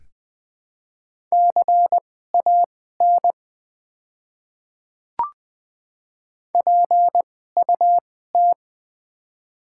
Morse code, heard as a keyed tone, about 20 wpm, spaced as for 15 wpm, 700 Hz. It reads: CAN PUT
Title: 2x Character Spacing - Top 200 Words - 20wpm